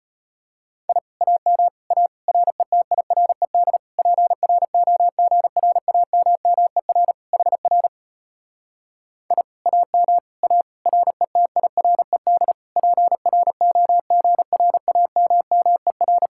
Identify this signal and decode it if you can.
Morse code, heard as a keyed tone, about 33 wpm, 700 Hz. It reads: I AM A RETIRED PROGRAMMER HR I AM A RETIRED PROGRAMMER